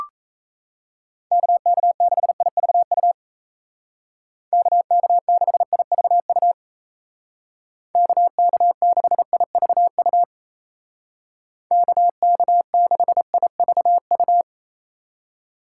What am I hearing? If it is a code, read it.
Morse code, heard as a keyed tone, about 34 wpm, 700 Hz. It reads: KK6IVU KK6IVU KK6IVU KK6IVU
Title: Speed Racing - USA 2x3 Call Signs 28wpm